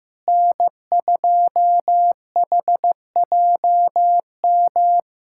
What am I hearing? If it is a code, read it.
Morse code, heard as a keyed tone, about 15 wpm, 700 Hz. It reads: N2HJM
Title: USA Callsign DE Callsign - 5 Characters - 15wpm